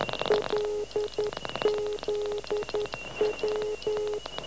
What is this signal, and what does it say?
Morse code, heard as a keyed tone, about 11 wpm, 420 Hz, with noise. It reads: LZW
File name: LZW 12880 1846